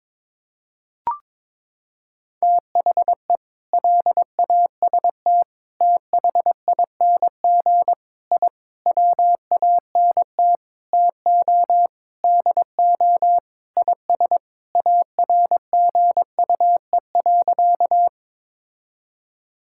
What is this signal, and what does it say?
Morse code, heard as a keyed tone, about 22 wpm, 700 Hz. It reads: THE LAST THING I WANT TO DO IS ARGUE.
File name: practice / 22wpm / Sentences with 7-Word Phrases - 22wpm